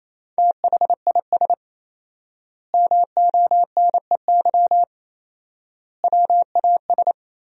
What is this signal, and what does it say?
Morse code, heard as a keyed tone, about 28 wpm, 700 Hz. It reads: THIS MONEY WAS
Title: 4x Word-Spacing - Sentences from Top 400 Words - 28wpm